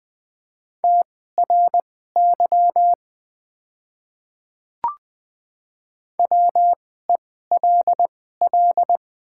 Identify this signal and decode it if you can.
Morse code, heard as a keyed tone, about 20 wpm, spaced as for 14 wpm, 700 Hz. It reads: TRY WELL
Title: 2x Character Spacing - Top 200 Words - 20wpm